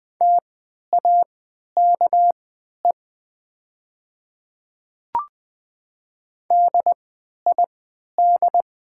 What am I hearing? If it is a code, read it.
Morse code, heard as a keyed tone, about 20 wpm, spaced as for 11 wpm, 700 Hz. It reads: TAKE DID
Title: 3x Character Spacing - Top 200 Words - 20wpm